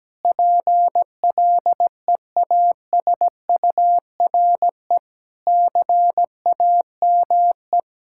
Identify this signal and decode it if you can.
Morse code, heard as a keyed tone, about 17 wpm, 700 Hz. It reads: PLEASURE CAME
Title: Sentences from Top 1600 Words 17wpm